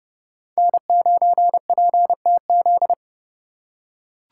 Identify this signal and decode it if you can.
Morse code, heard as a keyed tone, about 28 wpm, 700 Hz. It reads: N9PTZ